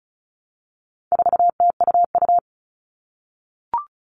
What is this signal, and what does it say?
Morse code, heard as a keyed tone, about 34 wpm, 700 Hz. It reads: 4TUU
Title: All Cut Numbers - 4-Digit 35wpm